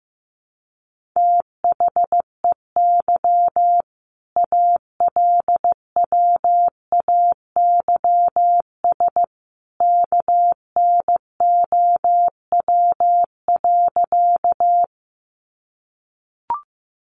Morse code, heard as a keyed tone, about 15 wpm, 700 Hz. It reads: THEY ALWAYS KNOW.
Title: Sentences from Top 300 Words 15wpm